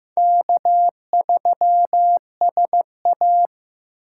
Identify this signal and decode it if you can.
Morse code, heard as a keyed tone, about 18 wpm, 700 Hz. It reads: K3SA
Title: Speed Racing - USA 1x2 Call Signs 15wpm